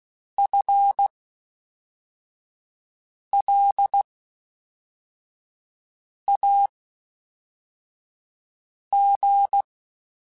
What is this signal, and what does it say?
Morse code, heard as a keyed tone, about 16 wpm, spaced as for 4 wpm, 800 Hz. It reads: FLAG
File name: Spaced / FL Runs sp mp3s / FL Runs sp @5